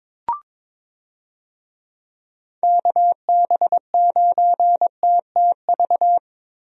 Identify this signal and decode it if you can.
Morse code, heard as a keyed tone, about 22 wpm, 700 Hz. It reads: KB9TTV
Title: USA Call Signs - 6 Characters - Repeated 3x - 22wpm